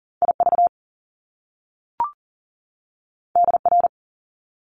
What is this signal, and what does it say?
Morse code, heard as a keyed tone, about 40 wpm, 700 Hz. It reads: IV DR
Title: Top Two Letter Combinations in Words 40wpm